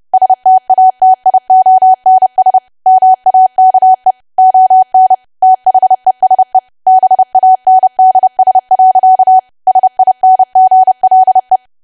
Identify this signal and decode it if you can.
Morse code, heard as a keyed tone, about 30 wpm, 750 Hz. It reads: STATIONS MAKE ON THESE BANDS. SINGLE
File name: cwa / W1AW-Level3 / 30WPM